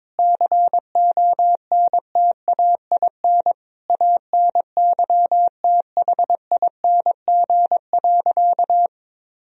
Morse code, heard as a keyed tone, about 22 wpm, 700 Hz. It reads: CONTAIN ANYTHING.
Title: Sentences from Top 1000 Words - Review Part 2 - 22wpm